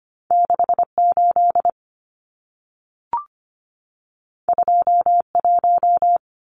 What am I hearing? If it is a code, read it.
Morse code, heard as a keyed tone, about 25 wpm, 700 Hz. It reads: 68 21
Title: Numbers - 2-Digit 25wpm